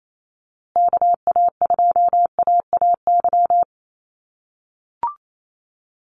Morse code, heard as a keyed tone, about 28 wpm, 700 Hz. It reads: KA2AAY
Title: USA Call Signs Repeated 3x 28wpm